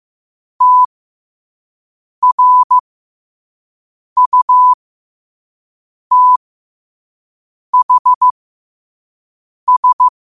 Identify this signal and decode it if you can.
Morse code, heard as a keyed tone, about 15 wpm, spaced as for 5 wpm, 1000 Hz. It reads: TRUTHS